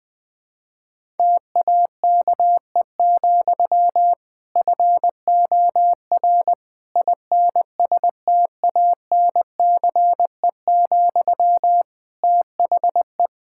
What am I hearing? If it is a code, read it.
Morse code, heard as a keyed tone, about 20 wpm, 700 Hz. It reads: TAKE, FOR INSTANCE, THE
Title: Sentences from Top 1000 Words - Review Part 5 - 20wpm